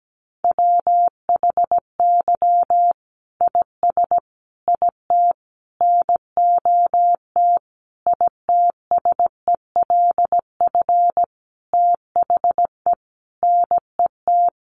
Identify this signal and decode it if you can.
Morse code, heard as a keyed tone, about 17 wpm, 700 Hz. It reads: WHY IS IT NOT ITSELF THE NET
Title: Sentences from Top 1400 Words 17wpm